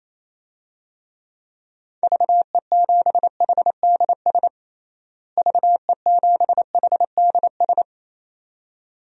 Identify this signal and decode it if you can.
Morse code, heard as a keyed tone, about 28 wpm, 700 Hz. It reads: VE7HDS VE7HDS
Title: DXCC-100 Call Signs - 6 Characters - Repeated 2x - 28wpm